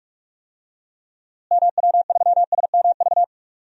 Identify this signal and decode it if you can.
Morse code, heard as a keyed tone, about 41 wpm, 700 Hz. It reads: MW3SMV
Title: DXCC-100 Call Signs - 6 Characters - Repeated 2x - 45wpm